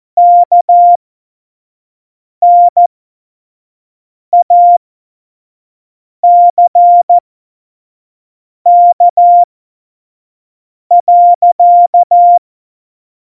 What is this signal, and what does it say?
Morse code, heard as a keyed tone, about 14 wpm, spaced as for 5 wpm, 700 Hz. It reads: KNACK.
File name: Morsekurs / Witze / Witz 13. November 2018, 25 F70